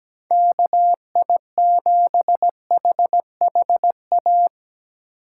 Morse code, heard as a keyed tone, about 17 wpm, 700 Hz. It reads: KI7HHA
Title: USA Call Signs - 6 Characters - 3x Character Spacing - 17wpm